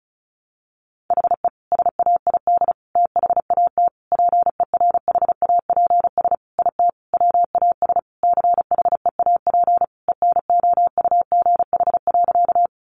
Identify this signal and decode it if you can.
Morse code, heard as a keyed tone, about 34 wpm, 700 Hz. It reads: HE SAID THAT PERHAPS IT WAS CHEAP ENOUGH.
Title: Sentences from Top 1200 Words 35wpm